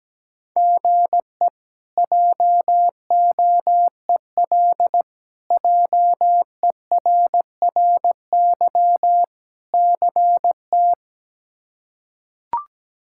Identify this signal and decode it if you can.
Morse code, heard as a keyed tone, about 17 wpm, 700 Hz. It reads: GE JOEL JERRY CT
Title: K1USN SST QSO Practice - 17wpm